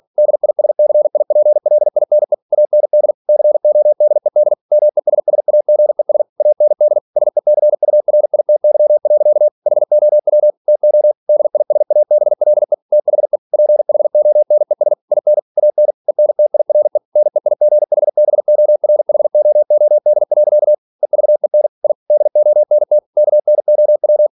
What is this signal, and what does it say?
Morse code, heard as a keyed tone, about 46 wpm, 600 Hz. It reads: DISCIPLINE AND CODED MESSAGES AND SECURITY. HOW TO DISABLE THE PHONES IN AN ENTIRE NEIGHBORHOOD. EVEN I DONT KNOW
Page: Never 47wpm